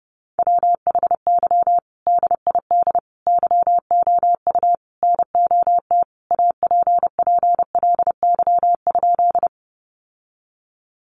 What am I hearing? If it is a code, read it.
Morse code, heard as a keyed tone, about 30 wpm, 700 Hz. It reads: WHY DID YOU NOT APPLY?
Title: Sentences from Top 1200 Words 30wpm